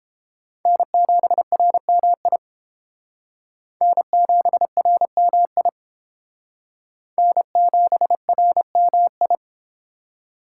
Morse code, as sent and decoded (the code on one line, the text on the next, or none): -. --... .-. -- .. / -. --... .-. -- .. / -. --... .-. -- ..
N7RMI N7RMI N7RMI